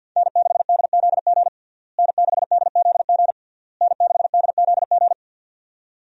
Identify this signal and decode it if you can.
Morse code, heard as a keyed tone, about 53 wpm, 700 Hz. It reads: N6DZG N6DZG N6DZG
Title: USA Call Signs - 5 Characters - Repeated 3x - 50wpm